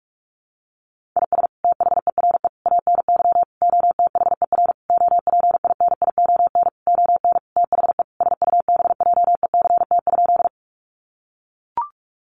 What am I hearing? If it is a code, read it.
Morse code, heard as a keyed tone, about 44 wpm, 700 Hz. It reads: IS THERE ANY OTHER OPINION ON THE SUBJECT?
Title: Sentences from Top 1200 Words 45wpm